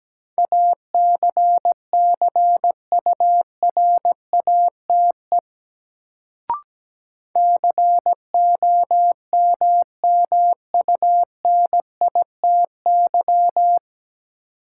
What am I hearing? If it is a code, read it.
Morse code, heard as a keyed tone, about 17 wpm, 700 Hz. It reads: ACCURATE COMMUNITY
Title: Sentences from Top 1200 Words - New Word Drill Alt - 17wpm